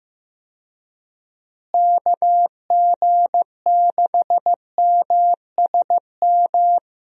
Morse code, heard as a keyed tone, about 15 wpm, 700 Hz. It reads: KG6MSM
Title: USA Call Signs - 6 Characters - Repeated 3x - 15wpm